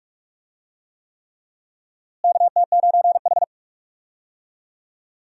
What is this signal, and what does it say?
Morse code, heard as a keyed tone, about 45 wpm, 700 Hz. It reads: KT9H
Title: USA Call Signs - 4 Characters - Repeated 3x - 45wpm